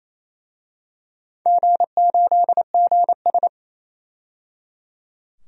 Morse code, heard as a keyed tone, about 28 wpm, 700 Hz. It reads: G8GS
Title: DXCC-100 Call Signs - 4 Characters - 28wpm